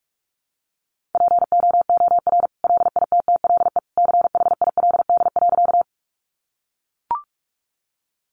Morse code, heard as a keyed tone, about 44 wpm, 700 Hz. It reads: POOR LITTLE CHILD.
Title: Sentences from Top 1000 Words 45wpm